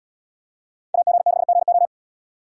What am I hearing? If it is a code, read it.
Morse code, heard as a keyed tone, about 70 wpm, 700 Hz. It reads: AG6GO